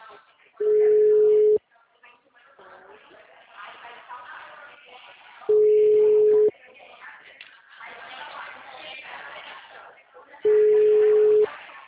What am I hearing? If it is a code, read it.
Ringback tone (Europe, 425 Hz), with speech.
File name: out- -2049- - - .4519